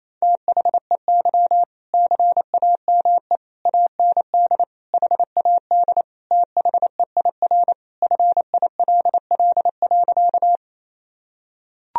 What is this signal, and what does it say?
Morse code, heard as a keyed tone, about 27 wpm, 700 Hz. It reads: THEY CAME AND HAD THEIR FILL.
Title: Sentences from Top 1000 Words - Review Part 3 - 28wpm